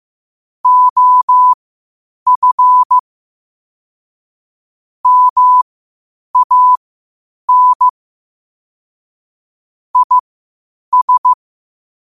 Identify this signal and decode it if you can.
Morse code, heard as a keyed tone, about 15 wpm, spaced as for 8 wpm, 1000 Hz. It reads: OF MAN IS
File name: qotdfiles / 8wpm